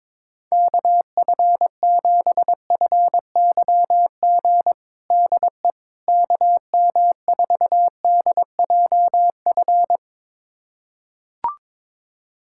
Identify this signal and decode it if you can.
Morse code, heard as a keyed tone, about 22 wpm, 700 Hz. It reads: KF7FYG DE KM4DJF
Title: USA Callsign DE Callsign 22wpm